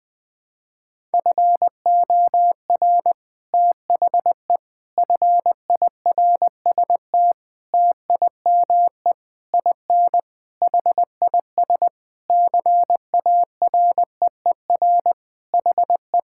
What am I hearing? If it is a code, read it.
Morse code, heard as a keyed tone, about 21 wpm, 700 Hz. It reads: FOR THE FIRST TIME IN HIS CAREER HE
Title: Sentences with 7-Word Phrases - Spoken Beginning - 20wpm